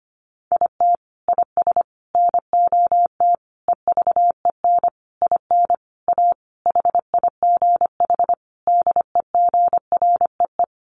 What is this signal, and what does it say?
Morse code, heard as a keyed tone, about 25 wpm, 700 Hz. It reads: IT IS NOT EVEN IN A HIGH DEGREE
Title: Sentences from Top 1500 Words 25wpm